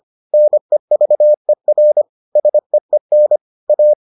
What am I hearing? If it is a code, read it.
Morse code, heard as a keyed tone, about 25 wpm, 600 Hz. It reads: NEVER SEEN A